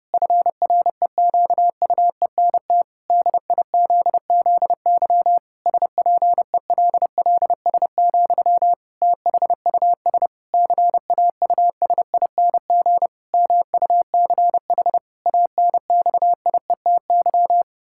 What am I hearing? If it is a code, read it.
Morse code, heard as a keyed tone, about 30 wpm, 700 Hz. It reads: FREQUENT DIZZY SPELLS, THUS CAUSING MUCH ANXIETY
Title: Pangram Challenge v2 - 30wpm